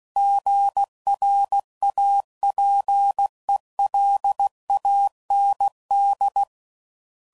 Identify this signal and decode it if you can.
Morse code, heard as a keyed tone, about 16 wpm, 800 Hz. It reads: GRAPELAND
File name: CW Code Groups @15WPM / fromSM's / Cities